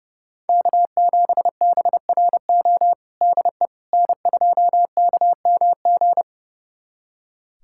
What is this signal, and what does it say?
Morse code, heard as a keyed tone, about 30 wpm, 700 Hz. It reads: K7BRO DE N2KMG
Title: USA Callsign DE Callsign - 5 Characters - 30wpm